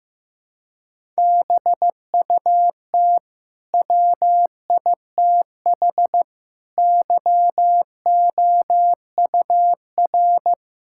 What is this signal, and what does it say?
Morse code, heard as a keyed tone, about 14 wpm, 700 Hz. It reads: BUT WITH YOUR
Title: Sentences from Top 1000 Words - Review Part 6 - 15wpm